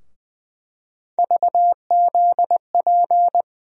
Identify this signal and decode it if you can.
Morse code, heard as a keyed tone, about 20 wpm, 700 Hz. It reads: VZP
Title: USA Call Sign Postfixes - 3 Letters 20wpm